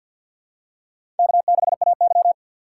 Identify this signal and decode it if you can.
Morse code, heard as a keyed tone, about 52 wpm, 700 Hz. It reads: K6AY
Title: USA Call Signs - 4 Characters - 3x Character Spacing - 50wpm